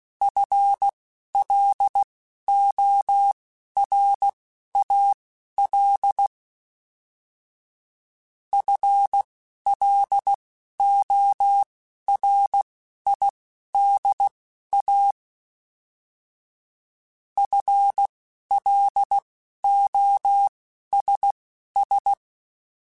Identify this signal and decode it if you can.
Morse code, heard as a keyed tone, about 16 wpm, spaced as for 11 wpm, 800 Hz. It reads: FLORAL FLORIDA FLOSS